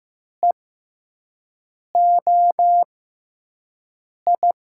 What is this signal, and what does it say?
Morse code, heard as a keyed tone, about 15 wpm, spaced as for 5 wpm, 700 Hz. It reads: EOI
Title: USA Call Signs - 6x Character Spacing - 15wpm